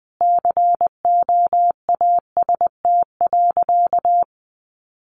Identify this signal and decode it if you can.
Morse code, heard as a keyed tone, about 20 wpm, 700 Hz. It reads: COAST.
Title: Sentences from Top 1300 Words 20wpm